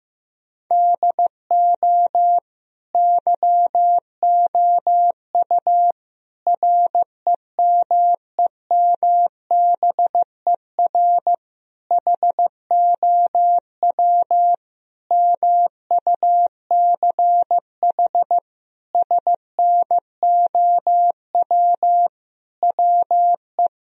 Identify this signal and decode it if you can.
Morse code, heard as a keyed tone, about 15 wpm, 700 Hz. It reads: DO YOU REMEMBER HOW MUCH SNOW WE
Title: Sentences from Top 1000 Words - Review Part 3 - 15wpm